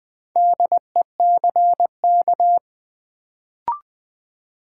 Morse code, heard as a keyed tone, about 20 wpm, 700 Hz. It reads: DECK
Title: Build A Word - 1 Letter At A Time - 4-Letter Words - 20wpm